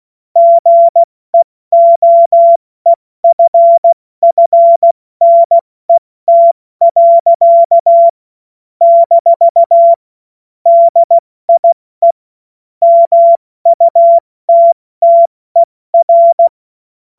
Morse code, heard as a keyed tone, about 16 wpm, 680 Hz. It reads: GEOEFFNET. - DIE MUTTER
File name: Morsekurs / Witze / Witz 21. September 2021, 70 F80